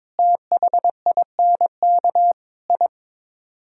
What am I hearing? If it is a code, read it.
Morse code, heard as a keyed tone, about 22 wpm, 700 Hz. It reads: THINK I
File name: practice / 22wpm / Build A Phrase - 5-Word Phrases - 22wpm